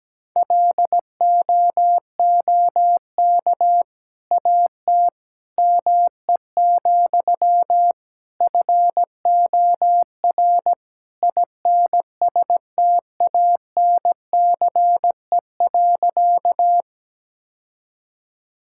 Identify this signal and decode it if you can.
Morse code, heard as a keyed tone, about 17 wpm, 700 Hz. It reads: LOOK AT ME, FOR INSTANCE.
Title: Sentences from Top 1000 Words - Review Part 5 - 17wpm